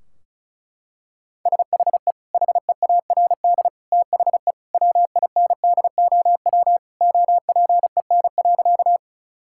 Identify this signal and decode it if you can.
Morse code, heard as a keyed tone, about 36 wpm, 700 Hz. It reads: SHE HEARD THE WINDOW OPEN.
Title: Sentences from Top 1600 Words 35wpm